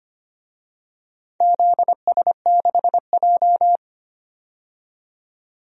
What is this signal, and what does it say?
Morse code, heard as a keyed tone, about 25 wpm, 700 Hz. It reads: ZS6J
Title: DXCC-100 Call Signs - 4 Characters - 25wpm